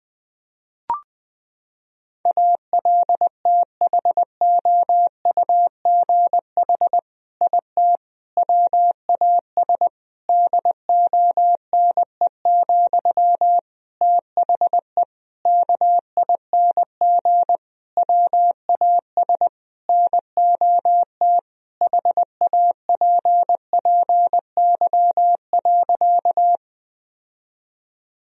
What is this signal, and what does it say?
Morse code, heard as a keyed tone, about 19 wpm, 700 Hz. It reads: ALTHOUGH IT WAS DONE, THE KING WAS NOT HAPPY.
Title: Sentences from Top 1000 Words - Review Part 3 - 20wpm